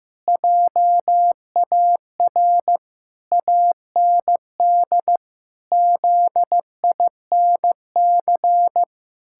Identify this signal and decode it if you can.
Morse code, heard as a keyed tone, about 14 wpm, 700 Hz. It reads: JAR AND ZINC
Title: Pangram Challenge v2 - 15wpm